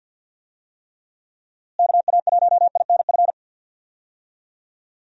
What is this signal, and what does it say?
Morse code, heard as a keyed tone, about 52 wpm, 700 Hz. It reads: KA1INF
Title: USA Call Signs - 6 Characters - Repeated 3x - 50wpm